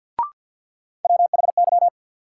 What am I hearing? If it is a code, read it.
Morse code, heard as a keyed tone, about 52 wpm, 700 Hz. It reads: WHY
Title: Sets of 3 Words - New Word Drill - 50wpm